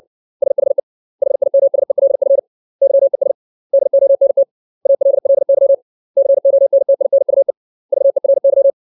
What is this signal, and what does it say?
Morse code, heard as a keyed tone, about 60 wpm, 550 Hz. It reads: SHE HIMSELF YES DONT ALLY CONTINUE VERY
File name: 632 words / 632 Words 60WPM